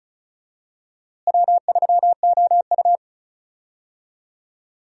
Morse code, heard as a keyed tone, about 34 wpm, 700 Hz. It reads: W3OU
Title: USA Call Signs - 4 Characters - Repeated 2x - 35wpm